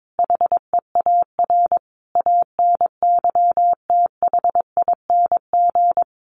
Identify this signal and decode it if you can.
Morse code, heard as a keyed tone, about 22 wpm, 700 Hz. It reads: HEAR ANYTHING
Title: Sentences from Top 1400 Words 22wpm